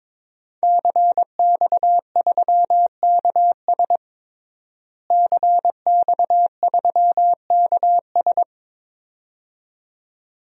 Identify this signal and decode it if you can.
Morse code, heard as a keyed tone, about 22 wpm, 700 Hz. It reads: CX3KS CX3KS
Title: DXCC-100 Call Signs - 5 Characters - Repeated 2x - 22wpm